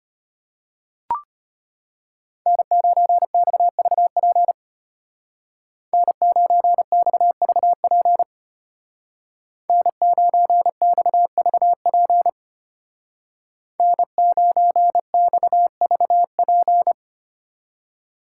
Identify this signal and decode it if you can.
Morse code, heard as a keyed tone, about 33 wpm, 700 Hz. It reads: N9XVP N9XVP N9XVP N9XVP